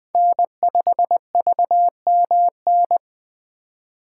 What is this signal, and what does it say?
Morse code, heard as a keyed tone, about 21 wpm, 700 Hz. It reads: N5VMN
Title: 2023 POTA Hunter Call Signs - 20wpm